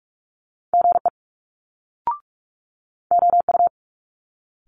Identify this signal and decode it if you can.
Morse code, heard as a keyed tone, about 46 wpm, spaced as for 11 wpm, 700 Hz. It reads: GEOU
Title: Multi-letter Phonemes Quick Fire 45wpm